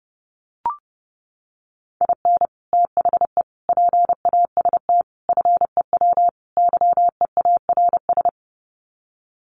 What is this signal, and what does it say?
Morse code, heard as a keyed tone, about 30 wpm, 700 Hz. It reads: IN THE PAST FEW YEARS
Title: Sets of 5 Words 30wpm